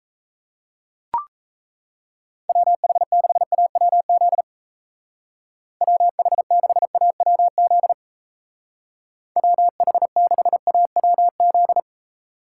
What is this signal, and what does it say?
Morse code, heard as a keyed tone, about 34 wpm, 700 Hz. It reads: WH6AWZ WH6AWZ WH6AWZ
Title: Speed Racing - USA 2x3 Call Signs 28wpm